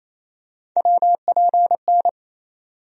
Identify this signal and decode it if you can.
Morse code, heard as a keyed tone, about 28 wpm, 700 Hz. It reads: WPN
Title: USA Call Sign Postfixes - 3 Letters 28wpm